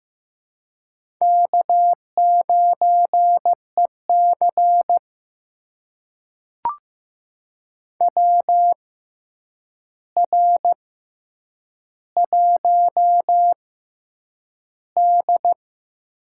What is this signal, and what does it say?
Morse code, heard as a keyed tone, about 15 wpm, spaced as for 8 wpm, 700 Hz. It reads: K9EC WR1D
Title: USA Call Signs - 4 Characters - 6x Character Spacing - 15wpm